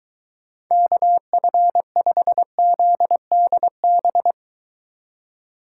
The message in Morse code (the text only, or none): KF5ZDB